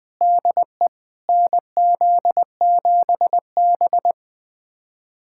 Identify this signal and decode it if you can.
Morse code, heard as a keyed tone, about 20 wpm, 700 Hz. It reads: DE NZ7B
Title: USA Callsign DE Callsign - 4 Characters - 20wpm